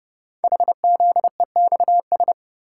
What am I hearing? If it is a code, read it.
Morse code, heard as a keyed tone, about 30 wpm, 700 Hz. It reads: HZEXS
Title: Code Groups - 5 Letters - 30wpm